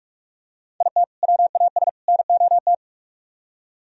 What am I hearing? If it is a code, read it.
Morse code, heard as a keyed tone, about 44 wpm, 700 Hz. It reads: IT WAS NOT